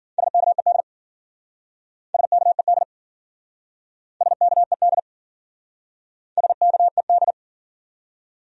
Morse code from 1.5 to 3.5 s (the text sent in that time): SKED